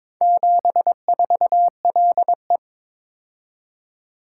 Morse code, heard as a keyed tone, about 22 wpm, 700 Hz. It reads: 74LE
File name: practice / 22wpm / Four Letter-Number Combinations - 22wpm